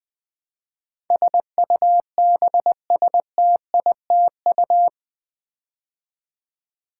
Morse code, heard as a keyed tone, about 20 wpm, 700 Hz. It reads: SUBSTITU
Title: Build A Word - 4 Letters At A Time - 12-Letter Words - 20wpm